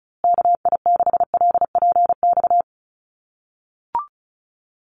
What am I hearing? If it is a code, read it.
Morse code, heard as a keyed tone, about 34 wpm, 700 Hz. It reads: KI6LPX
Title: USA Call Signs Repeated 3x 35wpm